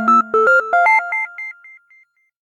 There is no code.